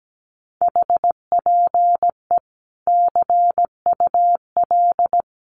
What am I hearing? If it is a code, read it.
Morse code, heard as a keyed tone, about 17 wpm, 700 Hz. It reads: HPE CUL
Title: Common QSO Phrases 17wpm